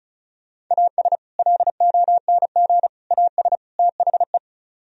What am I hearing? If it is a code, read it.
Morse code, heard as a keyed tone, about 46 wpm, spaced as for 41 wpm, 700 Hz. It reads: AS LONG AS THE
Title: Speed-Racing - Sets of 4 Words 35wpm